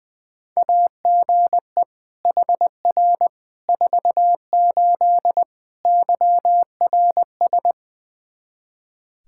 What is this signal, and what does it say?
Morse code, heard as a keyed tone, about 20 wpm, 700 Hz. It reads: AGE HR 48 YRS
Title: Practice QSO - Exchange 3 - More Info - 20wpm